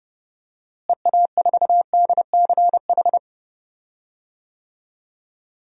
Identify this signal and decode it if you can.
Morse code, heard as a keyed tone, about 30 wpm, 700 Hz. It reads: EA4DCH
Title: DXCC-100 Call Signs - 6 Characters - Repeated 2x - 30wpm